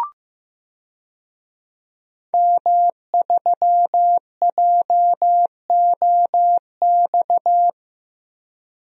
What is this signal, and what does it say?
Morse code, heard as a keyed tone, about 15 wpm, 700 Hz. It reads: M3JOX
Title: DXCC-100 Call Signs - 5 Characters - 15wpm